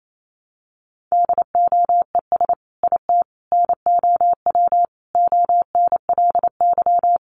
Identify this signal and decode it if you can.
Morse code, heard as a keyed tone, about 28 wpm, 700 Hz. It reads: DOES IT NOW ONLY
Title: Sentences from Top 1300 Words 28wpm